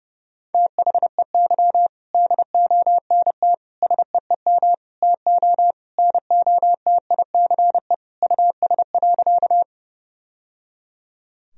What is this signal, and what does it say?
Morse code, heard as a keyed tone, about 32 wpm, 700 Hz. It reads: THEY DONT SEEM TO NOTICE US.
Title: Sentences from Top 1000 Words - Review Part 2 - 30wpm